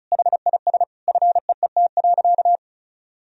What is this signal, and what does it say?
Morse code, heard as a keyed tone, about 34 wpm, 700 Hz. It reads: HIS FEET.
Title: Sentences from Top 1000 Words - Review Part 1 - 35wpm